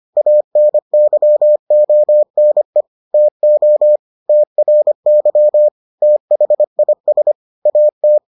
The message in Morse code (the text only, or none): ANYONE TO TRY THIS AT